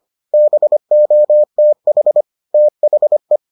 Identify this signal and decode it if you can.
Morse code, heard as a keyed tone, about 25 wpm, 600 Hz. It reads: BOTH THE